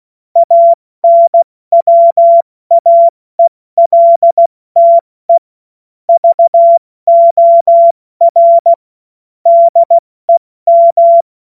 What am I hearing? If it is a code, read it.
Morse code, heard as a keyed tone, about 16 wpm, spaced as for 14 wpm, 680 Hz. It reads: ANWAELTE VOR DEM
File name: Morsekurs / Witze / Witz 20. September 2022 70 F80